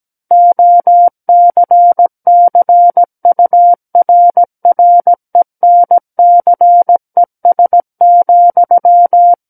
Morse code, heard as a keyed tone, about 17 wpm, 700 Hz. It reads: OCCURRENCES,
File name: Book One / 08-Chapter 8 - 17 WPM